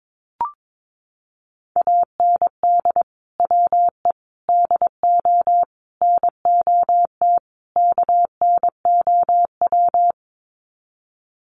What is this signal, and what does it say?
Morse code, heard as a keyed tone, about 22 wpm, 700 Hz. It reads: AND WE DO NOT KNOW
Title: Sets of 5 Words Encore 22wpm